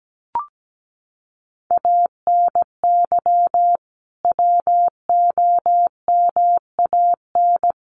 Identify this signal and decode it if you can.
Morse code, heard as a keyed tone, about 17 wpm, 700 Hz. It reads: ANY WOMAN
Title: Sentences from Top 1100 Words 17wpm